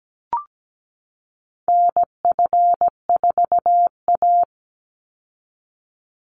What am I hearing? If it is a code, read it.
Morse code, heard as a keyed tone, about 17 wpm, 700 Hz. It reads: NF4A
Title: Call Signs Repeated 3x - 17wpm